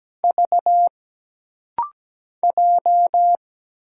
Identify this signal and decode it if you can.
Morse code, heard as a keyed tone, about 17 wpm, 700 Hz. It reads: V J
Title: Single Letter-Number Alt - Rapid-Fire 17wpm